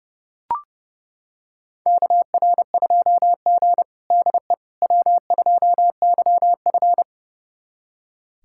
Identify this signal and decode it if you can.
Morse code, heard as a keyed tone, about 30 wpm, 700 Hz. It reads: KR2G DE W2YF
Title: USA Callsign DE Callsign - 4 Characters - 30wpm